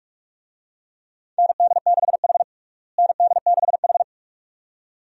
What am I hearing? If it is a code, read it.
Morse code, heard as a keyed tone, about 46 wpm, 700 Hz. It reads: ND6H ND6H
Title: USA Call Signs - 4 Characters - Repeated 2x - 45wpm